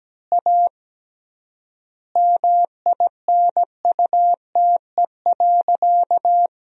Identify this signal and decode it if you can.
Morse code, heard as a keyed tone, about 17 wpm, 700 Hz. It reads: A MINUTE.
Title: 3x Word-Spacing - Sentences from Top 500 Words - 17wpm